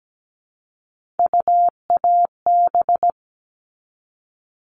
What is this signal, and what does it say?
Morse code, heard as a keyed tone, about 17 wpm, 700 Hz. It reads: UAB
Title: All Cut Numbers - 3-Digit 17wpm